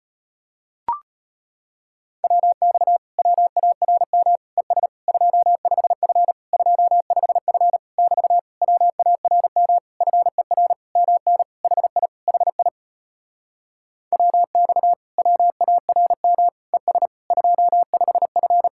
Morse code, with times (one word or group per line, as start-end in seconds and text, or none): 2.2-3.0 WX
3.2-4.4 WARM
4.6-4.9 ES
5.1-6.3 25F
6.5-7.8 25F
8.0-8.4 <BT>
8.6-9.8 WARM
10.0-10.7 FER
11.0-11.4 MN
11.6-12.1 HI
12.3-12.7 HI
14.1-14.9 WX
15.2-16.5 WARM
16.7-17.1 ES
17.3-18.7 25F